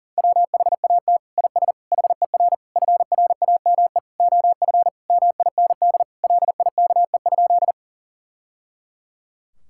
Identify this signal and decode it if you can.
Morse code, heard as a keyed tone, about 40 wpm, 700 Hz. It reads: WHAT IS HER FRAME OF MIND LIKE?
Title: Sentences from Top 1000 Words - Review Part 6 - 40wpm